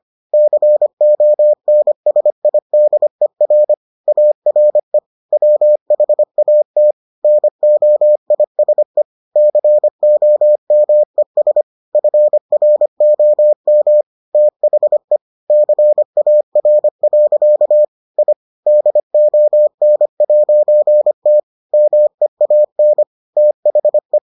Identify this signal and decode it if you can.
Morse code, heard as a keyed tone, about 25 wpm, 600 Hz. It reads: CONSIDER ARE WHAT NOISE COMES FROM THE CAR. I DON'T MEAN THE